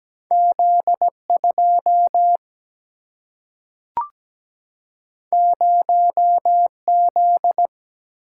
Morse code, heard as a keyed tone, about 17 wpm, 700 Hz. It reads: Z 2 0 Z
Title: Single-Letter Single-Number 17wpm